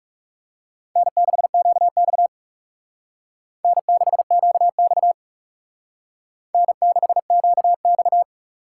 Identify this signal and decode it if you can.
Morse code, heard as a keyed tone, about 37 wpm, 700 Hz. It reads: N6QX N6QX N6QX
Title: Speed Racing - USA 1x2 Call Signs 30wpm